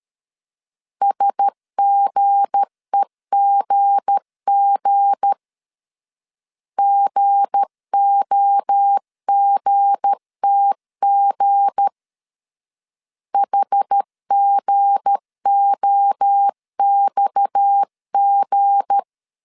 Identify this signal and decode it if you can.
Morse code, heard as a keyed tone, about 13 wpm, 800 Hz. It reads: SGEGG GOGTG HGOXG
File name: LEZIONI / 7 GF9Z / 7 G 12wpm 2055